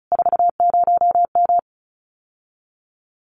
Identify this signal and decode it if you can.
Morse code, heard as a keyed tone, about 36 wpm, 700 Hz. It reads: 40M
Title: Common QSO Phrases Encore 35wpm